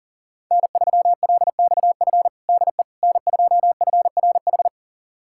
Morse code, heard as a keyed tone, about 40 wpm, 700 Hz. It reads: N3LXF DE N2FRH
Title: USA Callsign DE Callsign - 5 Characters - 40wpm